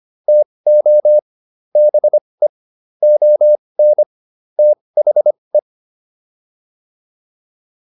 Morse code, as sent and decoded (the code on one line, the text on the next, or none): - --- / -... / --- -. / - .... .
TO BE ON THE